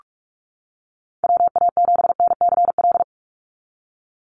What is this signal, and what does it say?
Morse code, heard as a keyed tone, about 44 wpm, 700 Hz. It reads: WA7NCL
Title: NA SOTA Activator Callsigns 45wpm